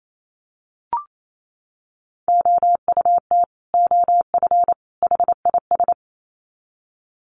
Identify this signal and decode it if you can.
Morse code, heard as a keyed tone, about 28 wpm, 700 Hz. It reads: OUT OF HIS
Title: Sets of 3 Words 28wpm